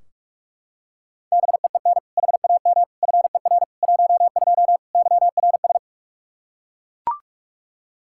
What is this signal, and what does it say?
Morse code, heard as a keyed tone, about 41 wpm, 700 Hz. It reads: BEEN HAM FER 12 YRS
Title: Practice QSO - Exchange 3 - More Info - 45wpm